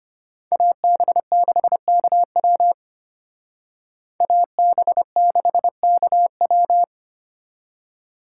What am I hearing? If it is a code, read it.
Morse code, heard as a keyed tone, about 31 wpm, 700 Hz. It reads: AB6KW AB6KW